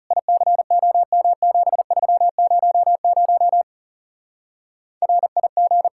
Morse code, heard as a keyed tone, about 38 wpm, 700 Hz. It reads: ICOM7300 RIG